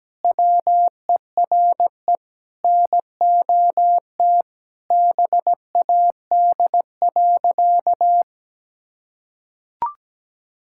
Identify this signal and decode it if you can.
Morse code, heard as a keyed tone, about 17 wpm, 700 Hz. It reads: WERE NOT BAD.
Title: Sentences from Top 1000 Words - Review Part 5 - 17wpm